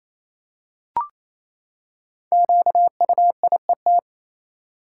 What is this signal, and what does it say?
Morse code, heard as a keyed tone, about 28 wpm, 700 Hz. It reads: QUIET
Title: Five Letter Words - 28wpm